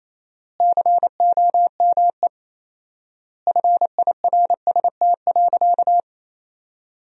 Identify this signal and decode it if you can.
Morse code, heard as a keyed tone, about 28 wpm, 700 Hz. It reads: COME FIRST.
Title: 4x Word-Spacing - Sentences from Top 300 Words - 28wpm